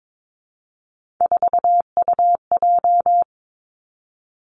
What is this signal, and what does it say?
Morse code, heard as a keyed tone, about 22 wpm, 700 Hz. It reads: 4UJ
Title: Three Challenging Letter-Number Combinations 22wpm